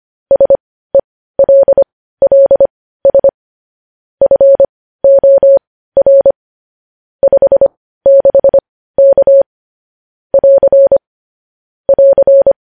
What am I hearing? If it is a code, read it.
Morse code, heard as a keyed tone, about 25 wpm, spaced as for 15 wpm, 550 Hz. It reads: SELLS FOR 56K <AR> <AR>
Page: news-headlines-in-morse-code-at-15-wpm